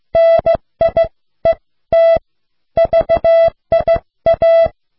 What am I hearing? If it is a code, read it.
Morse code, heard as a keyed tone, about 15 wpm, spaced as for 13 wpm, 660 Hz. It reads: NIET VIA